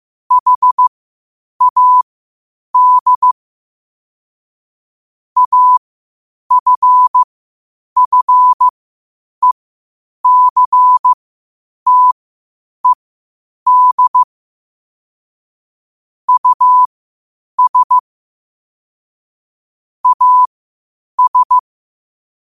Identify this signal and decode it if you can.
Morse code, heard as a keyed tone, about 15 wpm, spaced as for 8 wpm, 1000 Hz. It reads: HAD AFFECTED US AS
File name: qotdfiles / 8wpm